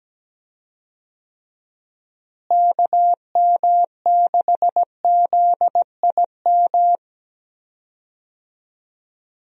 Morse code, heard as a keyed tone, about 17 wpm, 700 Hz. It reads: KM6ZIM
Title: USA Call Signs - 6 Characters - Repeated 2x - 17wpm